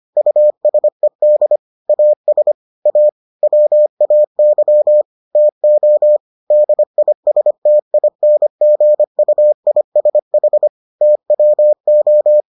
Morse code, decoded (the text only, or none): USED AS A WAY TO DISTINGUISH TWO